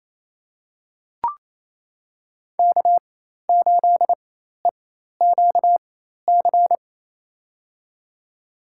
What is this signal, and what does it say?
Morse code, heard as a keyed tone, about 28 wpm, spaced as for 13 wpm, 700 Hz. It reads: K8EQC
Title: USA Call Signs - 5 Characters - 4x Character Spacing - 28wpm